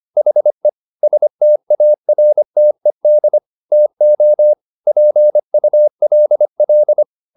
Morse code, heard as a keyed tone, about 25 wpm, 600 Hz. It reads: HE STARTED TO PULL